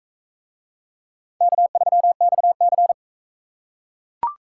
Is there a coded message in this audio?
Yes: Morse code from 1.4 to 2.9 s, heard as a keyed tone, about 42 wpm, 700 Hz. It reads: K3XC